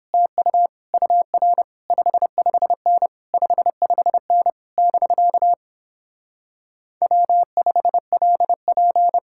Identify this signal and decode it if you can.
Morse code, heard as a keyed tone, about 31 wpm, 700 Hz. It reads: TU UR 55N 55N <BK> W5LP